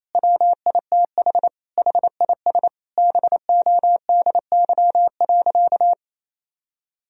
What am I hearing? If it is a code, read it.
Morse code, heard as a keyed tone, about 28 wpm, 700 Hz. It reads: WITH HIS BODY.
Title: Sentences from Top 1000 Words - Review Part 2 - 28wpm